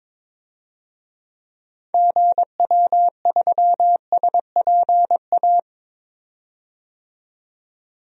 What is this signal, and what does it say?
Morse code, heard as a keyed tone, about 22 wpm, 700 Hz. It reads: GW3SPA
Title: DXCC-100 Call Signs - 6 Characters - 22wpm